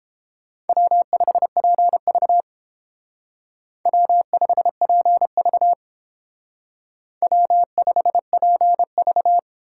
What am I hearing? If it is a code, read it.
Morse code, heard as a keyed tone, about 28 wpm, 700 Hz. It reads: W5PV W5PV W5PV